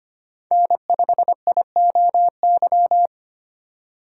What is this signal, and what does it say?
Morse code, heard as a keyed tone, about 25 wpm, 700 Hz. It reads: N5IOY
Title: USA Call Signs - 5 Characters - 2x Character Spacing - 25wpm